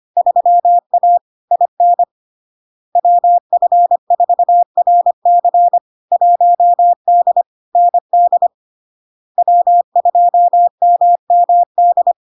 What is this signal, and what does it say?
Morse code, heard as a keyed tone, about 25 wpm, 700 Hz. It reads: 3A IN WF4RC 1D ND W2MMD
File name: Log-in-CW-25wpm-700-Hz